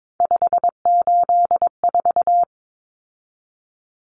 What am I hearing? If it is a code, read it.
Morse code, heard as a keyed tone, about 22 wpm, 700 Hz. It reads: 584
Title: Common Cut Numbers - 3-Digit 22wpm